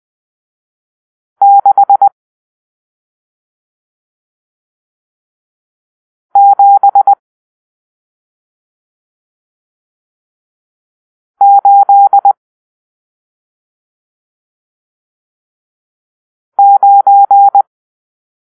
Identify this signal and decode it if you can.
Morse code, heard as a keyed tone, about 20 wpm, 800 Hz. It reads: 6 7 8 9
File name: arrl / Grp-7 Drill-1